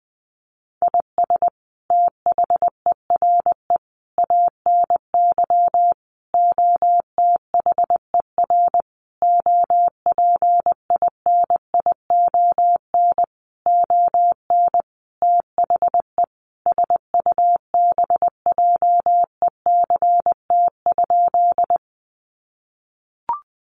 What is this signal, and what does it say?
Morse code, heard as a keyed tone, about 19 wpm, 700 Hz. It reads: IS THERE ANY OTHER OPINION ON THE SUBJECT?
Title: Sentences from Top 1200 Words 20wpm